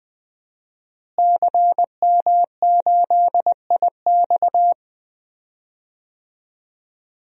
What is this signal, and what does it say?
Morse code, heard as a keyed tone, about 20 wpm, 700 Hz. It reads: CM8IX
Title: DXCC-100 Call Signs - 5 Characters - Repeated 2x - 20wpm